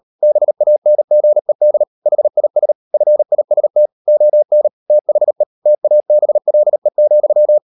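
Morse code, heard as a keyed tone, about 40 wpm, 600 Hz. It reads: BANGED HIS FIST ON THE TABLE,
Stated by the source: Morse code practice at 35–39 wpm sending English text of Ken Follett's novel Never, from CW med Gnister.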